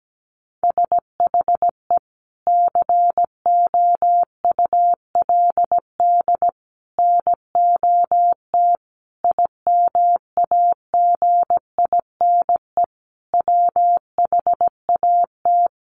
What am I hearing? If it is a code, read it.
Morse code, heard as a keyed tone, about 17 wpm, 700 Hz. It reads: SHE COULD NOT IMAGINE WHAT
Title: Sentences from Top 1400 Words 17wpm